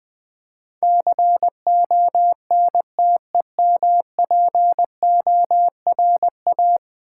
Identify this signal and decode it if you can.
Morse code, heard as a keyed tone, about 19 wpm, 700 Hz. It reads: CONTEMPORA
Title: Build A Word - 5 Letters At A Time - 12-Letter Words - 20wpm